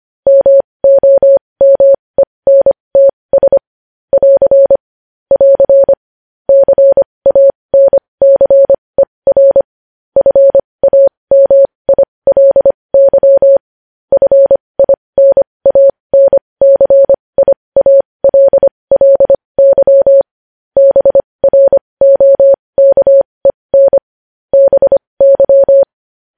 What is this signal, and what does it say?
Morse code, heard as a keyed tone, about 25 wpm, spaced as for 20 wpm, 550 Hz. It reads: MOMENTS <AR> <AR> CANCER FAMILY FINANCIALLY BROKEN BY
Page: news-headlines-in-morse-code-at-20-wpm